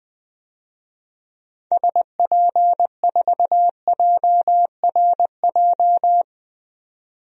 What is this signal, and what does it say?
Morse code, heard as a keyed tone, about 20 wpm, 700 Hz. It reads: SP4JRJ
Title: DXCC-100 Call Signs - 6 Characters - 20wpm